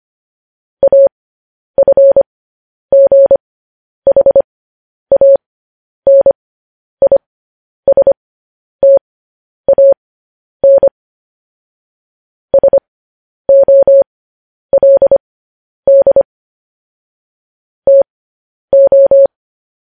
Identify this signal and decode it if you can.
Morse code, heard as a keyed tone, about 25 wpm, spaced as for 10 wpm, 550 Hz. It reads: AFGHANISTAN SOLD TO
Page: news headlines in morse code at 10 wpm